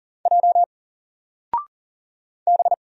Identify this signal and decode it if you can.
Morse code, heard as a keyed tone, about 40 wpm, 700 Hz. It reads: J B
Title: Single Character (Letter:Number:Punctuation:Prosign) 40wpm